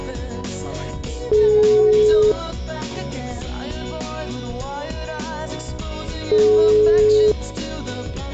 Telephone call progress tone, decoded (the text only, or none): ringback tone